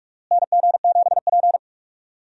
Morse code, heard as a keyed tone, about 48 wpm, 700 Hz. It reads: NG7P
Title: USA Call Signs - 4 Characters - Repeated 2x - 45wpm